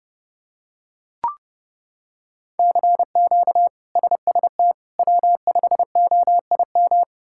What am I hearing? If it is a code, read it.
Morse code, heard as a keyed tone, about 30 wpm, 700 Hz. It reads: CQ SST W5OIM
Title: K1USN SST QSO Practice - 30wpm